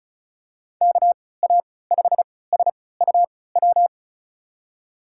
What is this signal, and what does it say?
Morse code, heard as a keyed tone, about 36 wpm, spaced as for 20 wpm, 700 Hz. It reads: KA5SUW
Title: USA Call Signs - 6 Characters - 3x Character Spacing - 35wpm